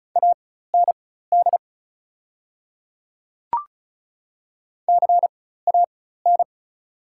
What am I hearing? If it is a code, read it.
Morse code, heard as a keyed tone, about 36 wpm, spaced as for 16 wpm, 690 Hz. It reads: AND CAN
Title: 4x Character Spacing - Top 300 Words - 35wpm